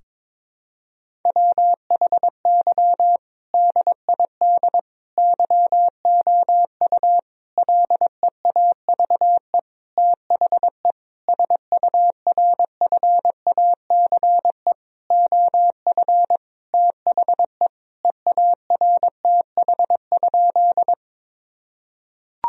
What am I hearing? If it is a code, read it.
Morse code, heard as a keyed tone, about 22 wpm, 700 Hz. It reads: WHY DID YOU LEAVE THE SURFACE OF THE EARTH?
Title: Sentences from Top 1000 Words - Review Part 3 - 22wpm